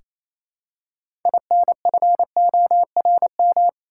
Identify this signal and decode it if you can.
Morse code, heard as a keyed tone, about 28 wpm, 700 Hz. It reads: INFORM
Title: Six Letter Words - 28wpm